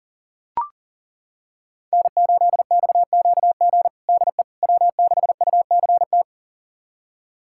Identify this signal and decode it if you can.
Morse code, heard as a keyed tone, about 40 wpm, 700 Hz. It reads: N8XQG DE W6UCT
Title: USA Callsign DE Callsign - 5 Characters - 40wpm